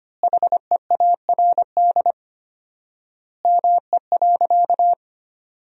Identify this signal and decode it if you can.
Morse code, heard as a keyed tone, about 25 wpm, 700 Hz. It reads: HEARD ME.
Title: 4x Word-Spacing - Sentences from Top 400 Words - 25wpm